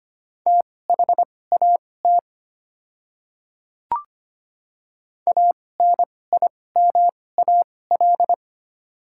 Morse code, heard as a keyed tone, about 25 wpm, spaced as for 18 wpm, 700 Hz. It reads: THAT ANIMAL
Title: 2x Character Spacing - Top 500 Words - 25wpm